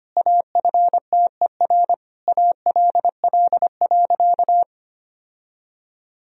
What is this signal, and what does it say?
Morse code, heard as a keyed tone, about 25 wpm, 700 Hz. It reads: AFTER ALL.
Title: Sentences from Top 1000 Words - Review Part 1 - 25wpm